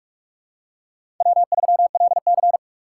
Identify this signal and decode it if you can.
Morse code, heard as a keyed tone, about 44 wpm, 700 Hz. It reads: W3LC
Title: USA Call Signs - 4 Characters - 6x Character Spacing - 45wpm